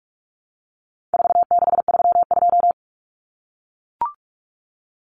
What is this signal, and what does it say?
Morse code, heard as a keyed tone, about 44 wpm, 700 Hz. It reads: 4632
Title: Common Cut Numbers - 4-Digit 45wpm